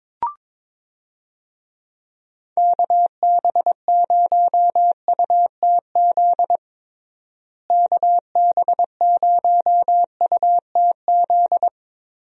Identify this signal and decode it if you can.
Morse code, heard as a keyed tone, about 22 wpm, 700 Hz. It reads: KB0UTZ KB0UTZ
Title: USA Call Signs - 6 Characters - Repeated 3x - 22wpm